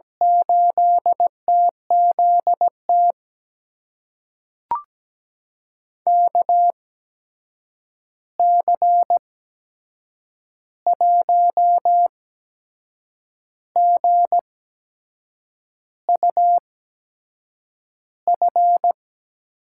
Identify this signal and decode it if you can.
Morse code, heard as a keyed tone, about 17 wpm, spaced as for 6 wpm, 700 Hz. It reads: 8TZT KC1GUF
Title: USA Call Signs - 6 Characters - 8x Character Spacing - 17wpm